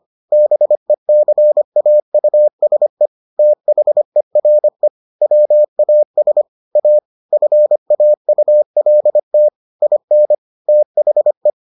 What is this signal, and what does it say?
Morse code, heard as a keyed tone, about 25 wpm, 600 Hz. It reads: BECAUSE THERE WAS A FAULT IN THE